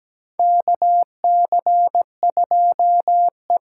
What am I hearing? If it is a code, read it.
Morse code, heard as a keyed tone, about 17 wpm, 700 Hz. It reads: KC2E
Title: USA Call Signs - 6 Characters - 8x Character Spacing - 17wpm